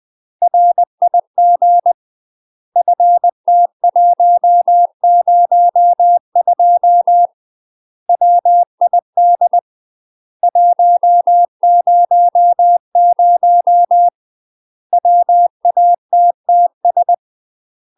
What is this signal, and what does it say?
Morse code, heard as a keyed tone, about 20 wpm, 700 Hz. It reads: RIG FT102 WID 100 WATTS